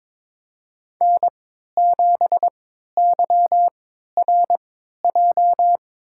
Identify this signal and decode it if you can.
Morse code, heard as a keyed tone, about 22 wpm, spaced as for 13 wpm, 700 Hz. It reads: N7YRJ
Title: USA Call Signs - 3x Character Spacing - 22wpm